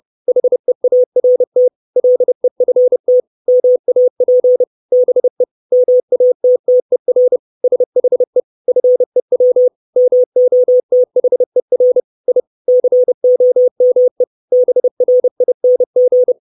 Morse code, heard as a keyed tone, about 30 wpm, 500 Hz. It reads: HEART LEFT MAP BE MATTER SHE FEW MOTHER I COME BRING
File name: 632 Words 30WPM